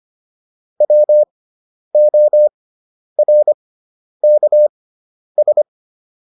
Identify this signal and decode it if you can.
Morse code, heard as a keyed tone, about 25 wpm, spaced as for 10 wpm, 600 Hz. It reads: WORKS